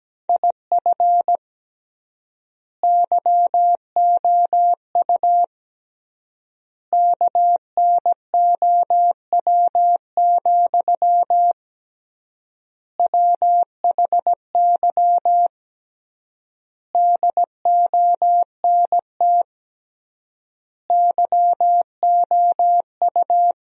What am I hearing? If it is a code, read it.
Morse code, heard as a keyed tone, about 17 wpm, 700 Hz. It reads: IF YOU KNOW, WHY DONT YOU
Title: 3x Word-Spacing - Sentences from Top 300 Words - 17wpm